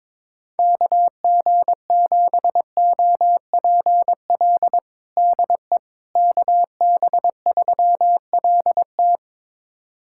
Morse code, heard as a keyed tone, about 22 wpm, 700 Hz. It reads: KG7OPL DE KB3LT
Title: USA Callsign DE Callsign 22wpm